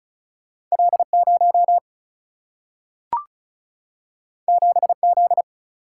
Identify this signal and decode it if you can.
Morse code, heard as a keyed tone, about 36 wpm, 700 Hz. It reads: L 0 7 Z
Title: Single-Letter Single-Number 35wpm